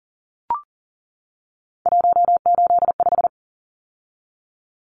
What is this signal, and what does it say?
Morse code, heard as a keyed tone, about 40 wpm, 700 Hz. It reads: 185
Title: Numbers - 3-Digit 40wpm